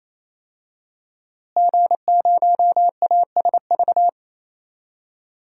Morse code, heard as a keyed tone, about 28 wpm, 700 Hz. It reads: G0ASV
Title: DXCC-100 Call Signs - 5 Characters - Repeated 3x - 28wpm